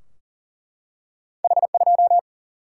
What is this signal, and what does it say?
Morse code, heard as a keyed tone, about 40 wpm, 700 Hz. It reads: H 2
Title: Single-Letter Single-Number 40wpm